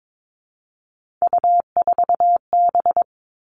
Morse code, heard as a keyed tone, about 22 wpm, 700 Hz. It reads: U4B
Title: Three Challenging Letter-Number Combinations 22wpm